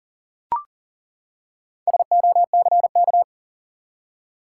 Morse code, heard as a keyed tone, about 40 wpm, 700 Hz. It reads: SOCK
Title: Four Letter Words - 40wpm